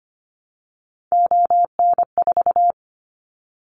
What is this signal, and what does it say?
Morse code, heard as a keyed tone, about 25 wpm, 700 Hz. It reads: ON4
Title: Callsign Prefixes for Top 25 DXCC - 25wpm